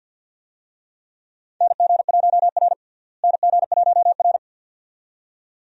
Morse code, heard as a keyed tone, about 51 wpm, 700 Hz. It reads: NG1R NG1R
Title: USA Call Signs - 4 Characters - Repeated 2x - 50wpm